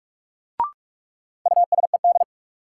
Morse code, heard as a keyed tone, about 46 wpm, 700 Hz. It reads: USED
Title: Sentences from Top 600 Words - New Word Drill - 45wpm